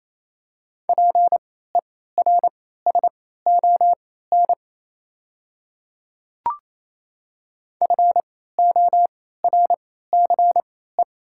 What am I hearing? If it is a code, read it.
Morse code, heard as a keyed tone, about 28 wpm, spaced as for 16 wpm, 700 Hz. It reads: PERSON FORCE
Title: 3x Character Spacing - Top 500 Words - 28wpm